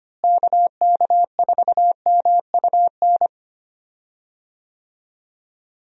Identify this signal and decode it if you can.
Morse code, heard as a keyed tone, about 25 wpm, 700 Hz. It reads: KK4MUN
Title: USA Call Signs - 6 Characters - Repeated 2x - 25wpm